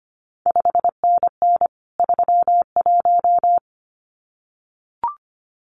Morse code, heard as a keyed tone, about 25 wpm, 700 Hz. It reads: 5NN 31
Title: CQ WW DX Contest Exchange - 25wpm